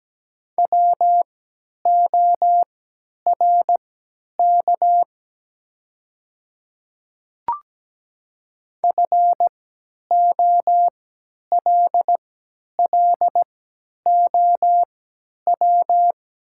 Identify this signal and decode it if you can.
Morse code, heard as a keyed tone, about 17 wpm, spaced as for 10 wpm, 700 Hz. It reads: WORK FOLLOW
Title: 3x Character Spacing - Top 300 Words - 17wpm